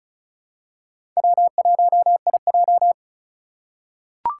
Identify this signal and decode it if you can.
Morse code, heard as a keyed tone, about 34 wpm, 700 Hz. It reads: W1IJ
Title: USA Call Signs - 4 Characters - 3x Character Spacing - 35wpm